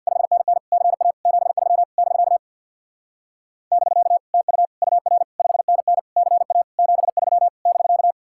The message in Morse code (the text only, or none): 5NN CA 73 <BK> <BK> TU UR 5NN CA 73 <BK>